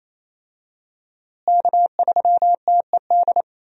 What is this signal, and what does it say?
Morse code, heard as a keyed tone, about 27 wpm, 700 Hz. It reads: K3TED
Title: USA Call Signs - 5 Characters - Repeated 2x - 28wpm